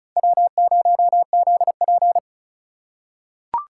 Morse code, heard as a keyed tone, about 34 wpm, 700 Hz. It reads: W0ZP
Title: K1USN SST QSO Practice - 35wpm